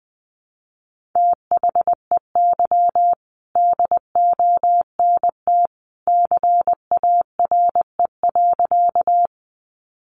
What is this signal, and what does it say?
Morse code, heard as a keyed tone, about 20 wpm, 700 Hz. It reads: THEY DONT CARE.
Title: Sentences from Top 300 Words 20wpm